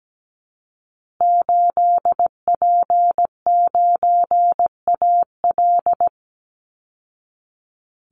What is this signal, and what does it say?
Morse code, heard as a keyed tone, about 17 wpm, 700 Hz. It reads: 8P9AL
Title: Call Signs Repeated 3x - 17wpm